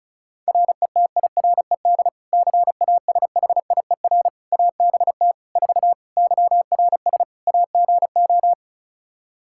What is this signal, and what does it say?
Morse code, heard as a keyed tone, about 34 wpm, 700 Hz. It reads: RETIRED CASHIER ABT 4 YRS AGO
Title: Practice QSO - Exchange 3 - More Info - 35wpm